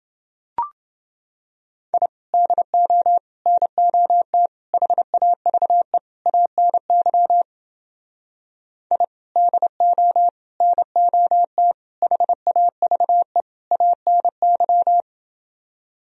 Morse code, heard as a keyed tone, about 30 wpm, 700 Hz. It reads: I DO NOT HAVE ANY I DO NOT HAVE ANY